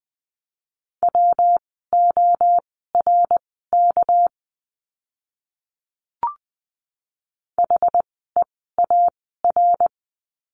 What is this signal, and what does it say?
Morse code, heard as a keyed tone, about 20 wpm, spaced as for 15 wpm, 700 Hz. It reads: WORK HEAR
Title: 244 Lesson 39f - Letter z - Words 20wpm